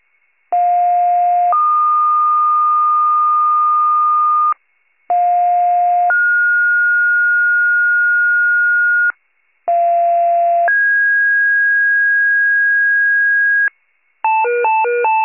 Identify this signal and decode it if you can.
Two-tone page, heard as a one-second tone, then a three-second tone, about 710 then 1220 Hz, about 710 then 1430 Hz, about 690 then 1650 Hz, with noise. It is heard from 0.5 to 13.7 s.
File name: wcalert